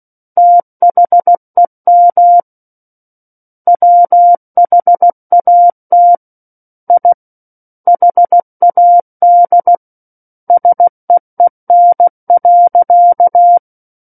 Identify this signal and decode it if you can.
Morse code, heard as a keyed tone, about 16 wpm, 700 Hz. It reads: THEM WHAT I HAD SEEN.
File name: Book One / 07-Chapter 7 - 16 WPM